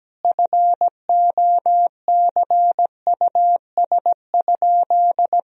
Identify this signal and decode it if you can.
Morse code, heard as a keyed tone, about 17 wpm, 700 Hz. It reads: FOCUS?
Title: Sentences from Top 1000 Words - Review Part 4 - 17wpm